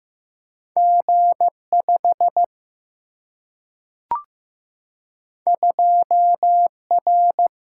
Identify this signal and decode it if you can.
Morse code, heard as a keyed tone, about 15 wpm, 700 Hz. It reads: G 5 2 R
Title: Single-Letter Single-Number 15wpm